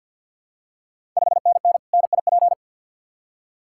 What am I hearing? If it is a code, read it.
Morse code, heard as a keyed tone, about 47 wpm, 690 Hz. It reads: 5NN NIP
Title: Ontario QSO Party Exchange 50wpm